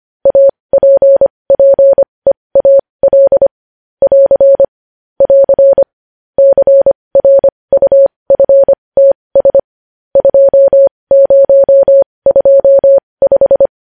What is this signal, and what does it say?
Morse code, heard as a keyed tone, about 25 wpm, spaced as for 20 wpm, 550 Hz. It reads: APPEAL <AR> <AR> CRUFTS 2025
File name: podcast / News headlines-1x-25x20